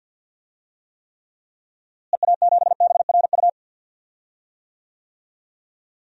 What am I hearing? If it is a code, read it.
Morse code, heard as a keyed tone, about 47 wpm, 700 Hz. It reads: EA7BRU
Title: DXCC-100 Call Signs - 6 Characters - Repeated 3x - 50wpm